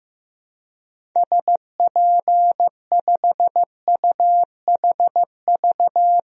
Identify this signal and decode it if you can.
Morse code, heard as a keyed tone, about 15 wpm, 700 Hz. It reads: SP5UHV
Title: DXCC-100 Call Signs - 6 Characters - 15wpm